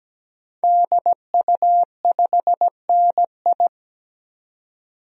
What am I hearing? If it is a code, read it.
Morse code, heard as a keyed tone, about 17 wpm, 700 Hz. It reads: DU5NI
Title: DXCC-100 Call Signs - 5 Characters - 17wpm